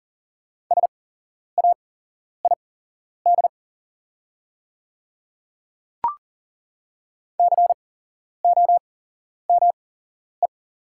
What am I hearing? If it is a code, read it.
Morse code, heard as a keyed tone, about 40 wpm, spaced as for 11 wpm, 700 Hz. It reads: SAID COME
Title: 8x Character Spacing - Top 200 Words - 40wpm